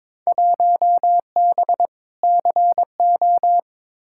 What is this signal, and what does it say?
Morse code, heard as a keyed tone, about 22 wpm, 700 Hz. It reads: 1B CO
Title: Field Day Exchanges - 22wpm